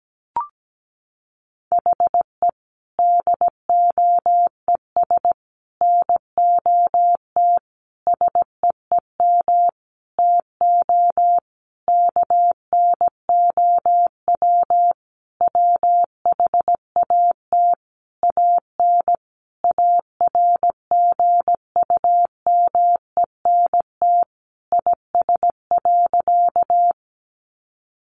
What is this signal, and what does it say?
Morse code, heard as a keyed tone, about 17 wpm, 700 Hz. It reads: HE DOES NOT SEEM TO KNOW WHAT AN ARGUMENT IS.
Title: Sentences from Top 1400 Words 17wpm